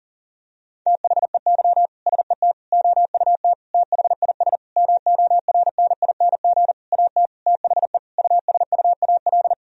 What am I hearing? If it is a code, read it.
Morse code, heard as a keyed tone, about 40 wpm, 700 Hz. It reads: THEY SET OUT THIS MORNING AT THE USUAL
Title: Sentences from Top 1000 Words - Review Part 2 - 40wpm